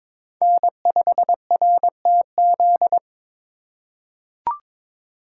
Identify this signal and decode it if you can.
Morse code, heard as a keyed tone, about 21 wpm, 700 Hz. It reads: N5RTZ
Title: USA Call Signs - 4x Character Spacing - 22wpm